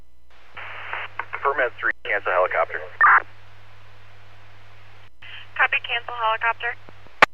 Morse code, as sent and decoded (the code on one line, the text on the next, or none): none